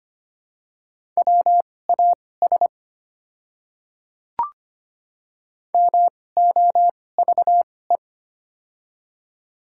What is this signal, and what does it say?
Morse code, heard as a keyed tone, about 25 wpm, spaced as for 18 wpm, 700 Hz. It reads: WAS MOVE
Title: 2x Character Spacing - Top 300 Words - 25wpm